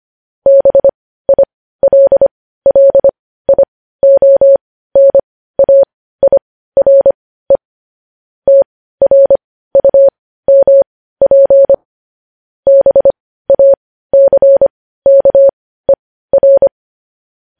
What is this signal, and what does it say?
Morse code, heard as a keyed tone, about 25 wpm, spaced as for 15 wpm, 550 Hz. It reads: BILLIONAIRE TRUMP BACKER
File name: podcast / News headlines-1x-25x15